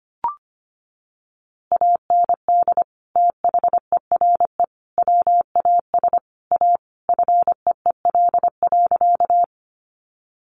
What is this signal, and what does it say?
Morse code, heard as a keyed tone, about 25 wpm, 700 Hz. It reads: AND THERE WAS A FEEL.
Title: A Sound Of Thunder 25wpm